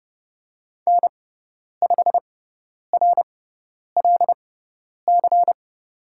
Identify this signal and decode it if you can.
Morse code, heard as a keyed tone, about 30 wpm, spaced as for 10 wpm, 700 Hz. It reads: N5RLC
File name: Lesson 17g - Letter u - Call Signs 30wpm